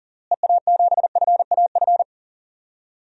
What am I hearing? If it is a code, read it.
Morse code, heard as a keyed tone, about 37 wpm, 700 Hz. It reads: EA7FAF
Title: DXCC-100 Call Signs - 6 Characters - Repeated 3x - 40wpm